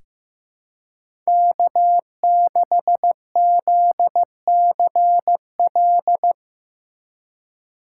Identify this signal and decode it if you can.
Morse code, heard as a keyed tone, about 15 wpm, 700 Hz. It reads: K6ZCL
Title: K1USN SST QSO Practice - 15wpm